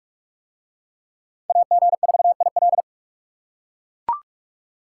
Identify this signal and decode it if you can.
Morse code, heard as a keyed tone, about 40 wpm, 700 Hz. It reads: AG4IL